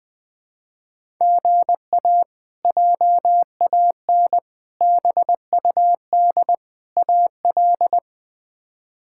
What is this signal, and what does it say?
Morse code, heard as a keyed tone, about 20 wpm, 700 Hz. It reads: GA JAN BUD AL
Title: K1USN SST QSO Practice - 20wpm